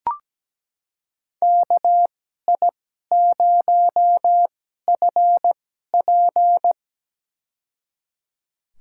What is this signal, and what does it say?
Morse code, heard as a keyed tone, about 17 wpm, spaced as for 12 wpm, 700 Hz. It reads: KI0FP
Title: USA Call Signs - 5 Characters - 2x Character Spacing - 17wpm